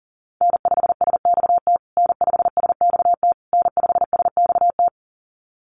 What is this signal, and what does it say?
Morse code, heard as a keyed tone, about 37 wpm, 700 Hz. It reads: N5SXT N5SXT N5SXT
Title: USA Call Signs Repeated 3x 40wpm